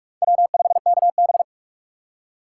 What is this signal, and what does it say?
Morse code, heard as a keyed tone, about 46 wpm, 690 Hz. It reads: W5KB
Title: USA Call Signs - 4x Character Spacing - 45wpm